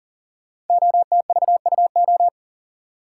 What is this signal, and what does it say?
Morse code, heard as a keyed tone, about 42 wpm, 700 Hz. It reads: OTVUO
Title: Code Groups - 5 Letter:Numbers - 40wpm